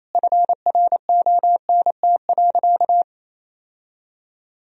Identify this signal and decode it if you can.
Morse code, heard as a keyed tone, about 28 wpm, 700 Hz. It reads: FRONT.
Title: Sentences from Top 1000 Words - Review Part 2 - 28wpm